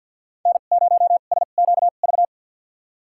Morse code, heard as a keyed tone, about 51 wpm, spaced as for 36 wpm, 700 Hz. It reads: N0SQV
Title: USA Call Signs - 5 Characters - 2x Character Spacing - 50wpm